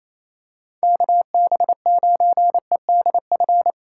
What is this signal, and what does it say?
Morse code, heard as a keyed tone, about 28 wpm, 700 Hz. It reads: KB9EDF
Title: USA Call Signs - 6 Characters - 4x Character Spacing - 28wpm